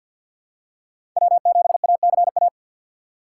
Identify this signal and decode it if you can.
Morse code, heard as a keyed tone, about 53 wpm, 700 Hz. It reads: W7ACA
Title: USA Call Signs - 4x Character Spacing - 50wpm